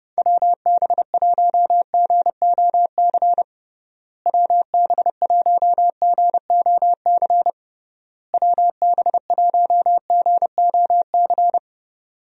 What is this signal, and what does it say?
Morse code, heard as a keyed tone, about 32 wpm, 700 Hz. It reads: WB1GOC WB1GOC WB1GOC
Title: USA Call Signs - 6 Characters - Repeated 3x - 30wpm